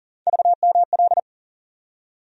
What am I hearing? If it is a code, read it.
Morse code, heard as a keyed tone, about 40 wpm, 700 Hz. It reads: VML
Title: USA Call Sign Postfixes - 3 Letters 40wpm